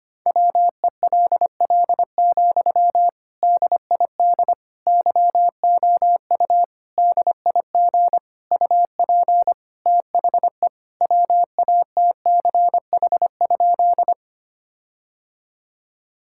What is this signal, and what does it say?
Morse code, heard as a keyed tone, about 25 wpm, 700 Hz. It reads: WELL, DID YOU DIG UP THE WATCH?
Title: Sentences from Top 1600 Words 25wpm